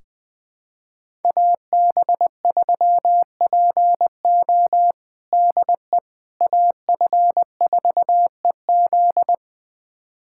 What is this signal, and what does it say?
Morse code, heard as a keyed tone, about 21 wpm, 700 Hz. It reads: AB3PO DE AF4EZ
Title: USA Callsign DE Callsign - 5 Characters - 20wpm